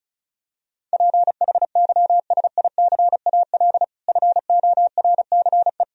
Morse code, heard as a keyed tone, about 34 wpm, 700 Hz. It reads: PHYSICAL FORCE
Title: Sentences from Top 1000 Words - Review Part 4 - 35wpm